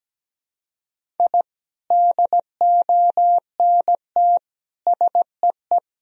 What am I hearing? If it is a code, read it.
Morse code, heard as a keyed tone, about 17 wpm, 700 Hz. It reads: I DONT SEE
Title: Sentences from Top 1000 Words - Review Part 4 - 17wpm